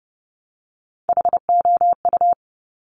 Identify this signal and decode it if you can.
Morse code, heard as a keyed tone, about 30 wpm, 700 Hz. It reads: HOU
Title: Top Three Letter Combinations in Words 30wpm